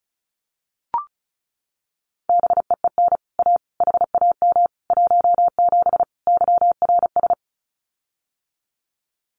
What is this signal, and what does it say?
Morse code, heard as a keyed tone, about 34 wpm, 700 Hz. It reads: BEEN A HAM 17 YRS
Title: Common QSO Phrases Encore 35wpm